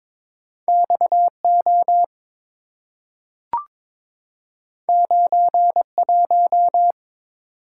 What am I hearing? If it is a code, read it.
Morse code, heard as a keyed tone, about 22 wpm, 690 Hz. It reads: XO 91
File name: practice / 22wpm / Two Letter-Number Combinations - 22wpm